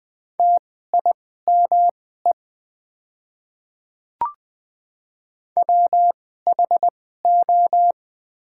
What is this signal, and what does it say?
Morse code, heard as a keyed tone, about 20 wpm, spaced as for 14 wpm, 700 Hz. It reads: TIME WHO
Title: 2x Character Spacing - Top 100 Words - 20wpm